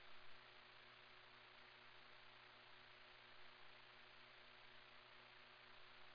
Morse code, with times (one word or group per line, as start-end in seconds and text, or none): none